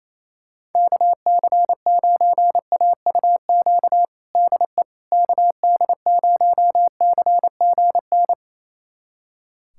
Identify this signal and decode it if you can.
Morse code, heard as a keyed tone, about 28 wpm, 700 Hz. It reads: KC9AUQ DE KD0CGN
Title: USA Callsign DE Callsign - 6 Characters - 28wpm